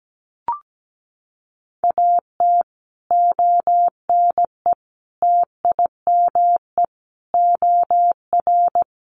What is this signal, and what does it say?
Morse code, heard as a keyed tone, about 17 wpm, 700 Hz. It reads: AT ONE TIME OR
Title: Sets of 5 Words Encore 17wpm